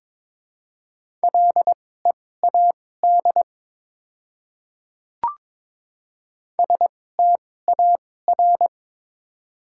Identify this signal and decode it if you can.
Morse code, heard as a keyed tone, about 22 wpm, spaced as for 16 wpm, 700 Hz. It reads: LEAD STAR
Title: 2x Character Spacing - Top 500 Words - 22wpm